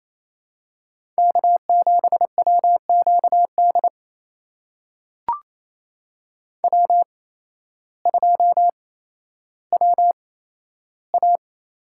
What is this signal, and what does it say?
Morse code, heard as a keyed tone, about 28 wpm, spaced as for 13 wpm, 700 Hz. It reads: K7WQD W2WA
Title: USA Call Signs - 8x Character Spacing - 28wpm